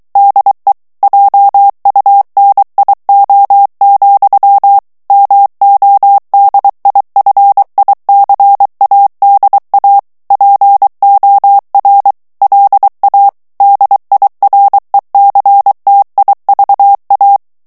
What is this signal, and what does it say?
Morse code, heard as a keyed tone, about 23 wpm, 800 Hz. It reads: DE JUNIO, MODIFICADA POR LA DIRECTIVA
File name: morse mp3 / ESPAGNOL TEXTE EN CLAIR 2 VITESSE 22 MOTS MINUTES